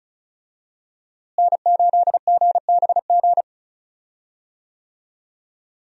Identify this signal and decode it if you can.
Morse code, heard as a keyed tone, about 36 wpm, 700 Hz. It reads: N8GBG
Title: USA Call Signs - 5 Characters - Repeated 3x - 35wpm